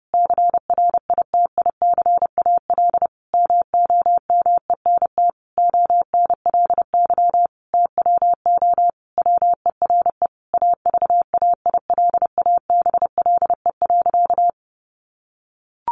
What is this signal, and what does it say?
Morse code, heard as a keyed tone, about 30 wpm, 700 Hz. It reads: CRITICAL MOMENT ONLY TWO WERE AVAILABLE.
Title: Sentences from Top 1100 Words 30wpm